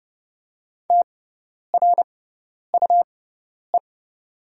Morse code, heard as a keyed tone, about 30 wpm, spaced as for 10 wpm, 700 Hz. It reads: TRUE
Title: 6x Character Spacing - Top 500 Words - 30wpm